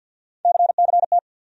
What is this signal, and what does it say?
Morse code, heard as a keyed tone, about 51 wpm, 700 Hz. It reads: CCT
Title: USA Call Sign Postfixes - 3 Letters 50wpm